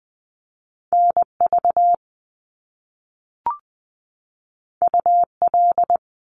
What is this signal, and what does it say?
Morse code, heard as a keyed tone, about 20 wpm, 700 Hz. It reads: NV UL
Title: Top Two Letter Combinations in Words 20wpm